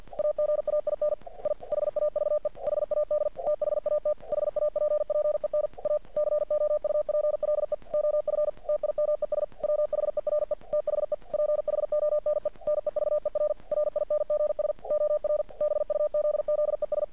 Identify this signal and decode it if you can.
Morse code, heard as a keyed tone, about 50 wpm, 600 Hz. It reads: AGAIN I HAVE HAD THAT HAPPEN A COUPLE OF TIMES WHERE THE PHONE NEVER RINGS OR BUZZES
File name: archived / CRSnet